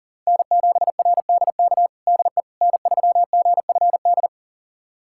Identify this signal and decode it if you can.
Morse code, heard as a keyed tone, about 40 wpm, 700 Hz. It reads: N7RDK DE N3GFD
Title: USA Callsign DE Callsign - 5 Characters - 40wpm